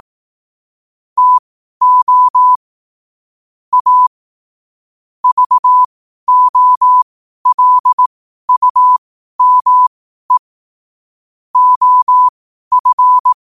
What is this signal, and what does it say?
Morse code, heard as a keyed tone, about 18 wpm, spaced as for 12 wpm, 1000 Hz. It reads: TO A VOLUME OF
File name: qotdfiles / 12wpm